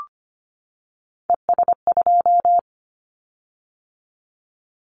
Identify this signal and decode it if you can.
Morse code, heard as a keyed tone, about 25 wpm, 700 Hz. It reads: ES2
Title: Callsign Prefixes for Top 100 DXCC - 25wpm